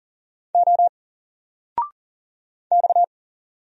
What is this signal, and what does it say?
Morse code, heard as a keyed tone, about 42 wpm, 700 Hz. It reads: O X
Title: Single Character (Letter:Number:Punctuation:Prosign) 40wpm